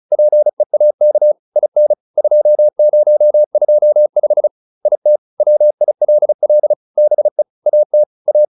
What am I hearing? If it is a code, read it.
Morse code, heard as a keyed tone, about 34 wpm, 600 Hz. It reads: PEAK IN 2025 IT WILL BE AT A